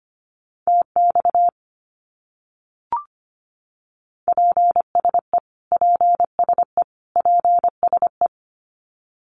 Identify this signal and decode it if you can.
Morse code, heard as a keyed tone, about 25 wpm, 700 Hz. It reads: TX PSE PSE PSE
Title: Abbreviations Repeated 3x - 25wpm